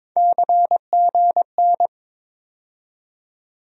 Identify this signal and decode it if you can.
Morse code, heard as a keyed tone, about 22 wpm, 700 Hz. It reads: CGN
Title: USA Call Sign Postfixes - 3 Letters 22wpm